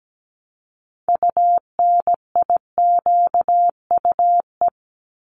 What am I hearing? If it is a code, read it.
Morse code, heard as a keyed tone, about 17 wpm, 700 Hz. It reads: UNIQUE
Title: Top 751-1000 Words 17wpm